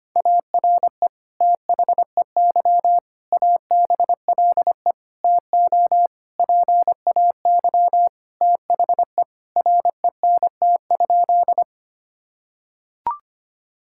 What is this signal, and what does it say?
Morse code, heard as a keyed tone, about 25 wpm, 700 Hz. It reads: ARE THEY ABLE TO PAY THE RENT?
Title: Sentences from Top 1000 Words - Review Part 5 - 25wpm